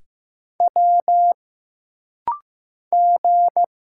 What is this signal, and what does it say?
Morse code, heard as a keyed tone, about 15 wpm, 700 Hz. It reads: W G
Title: Single Letters - Rapid-Fire 15wpm